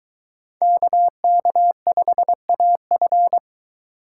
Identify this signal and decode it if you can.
Morse code, heard as a keyed tone, about 23 wpm, 700 Hz. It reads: KK5AF